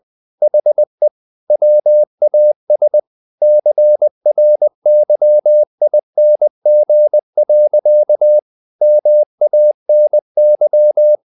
Morse code, heard as a keyed tone, about 20 wpm, 600 Hz. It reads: HE WAS CRYING. MANY
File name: the old man and the sea part 2 20wpm